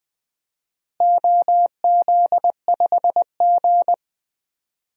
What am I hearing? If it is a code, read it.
Morse code, heard as a keyed tone, about 20 wpm, 700 Hz. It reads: OZ5G
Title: DXCC-100 Call Signs - 4 Characters - Repeated 3x - 20wpm